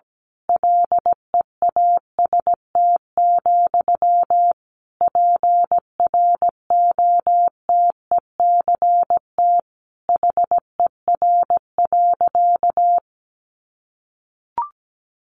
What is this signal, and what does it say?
Morse code, heard as a keyed tone, about 17 wpm, 700 Hz. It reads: LEAST, PROTECT HER.
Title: Sentences from Top 900 Words 17wpm